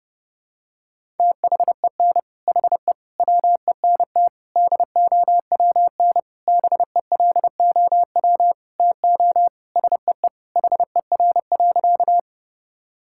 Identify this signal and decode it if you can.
Morse code, heard as a keyed tone, about 30 wpm, 700 Hz. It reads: THEN HE WENT DOWN BELOW TO SEE HER.
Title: Sentences from Top 1000 Words - Review Part 5 - 30wpm